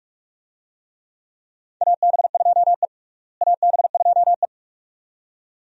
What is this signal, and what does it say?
Morse code, heard as a keyed tone, about 47 wpm, 700 Hz. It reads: AB2E AB2E
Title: USA Call Signs - 4 Characters - Repeated 2x - 45wpm